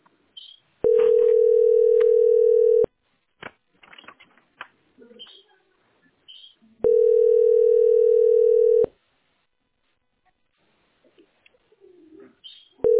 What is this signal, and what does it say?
Ringback tone (North America), 440 + 480 Hz.